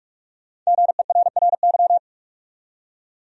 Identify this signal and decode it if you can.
Morse code, heard as a keyed tone, about 46 wpm, 700 Hz. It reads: GERRY
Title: USA Common Names - Max 5 Letters - 45wpm